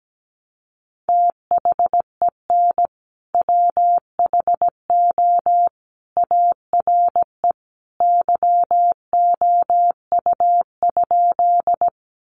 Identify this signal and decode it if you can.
Morse code, heard as a keyed tone, about 17 wpm, 700 Hz. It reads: THEN WHO ARE YOU?
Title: Sentences from Top 100 Words 17wpm